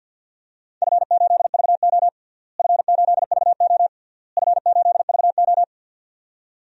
Morse code, heard as a keyed tone, about 50 wpm, 700 Hz. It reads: F8VO F8VO F8VO
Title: DXCC-100 Call Signs - 4 Characters - Repeated 3x - 50wpm